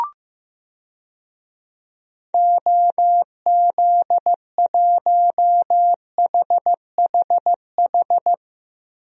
Morse code, heard as a keyed tone, about 15 wpm, 700 Hz. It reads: OZ1HHH
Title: DXCC-100 Call Signs - 6 Characters - Repeated 2x - 15wpm